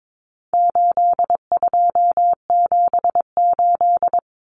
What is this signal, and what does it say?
Morse code, heard as a keyed tone, about 22 wpm, 700 Hz. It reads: 8278
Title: Numbers - 4-Digit 22wpm